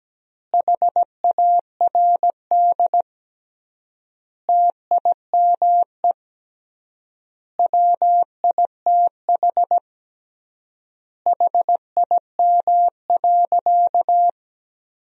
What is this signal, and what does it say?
Morse code, heard as a keyed tone, about 17 wpm, 700 Hz. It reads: HARD TIME WITH HIM.
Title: 3x Word-Spacing - Sentences from Top 300 Words - 17wpm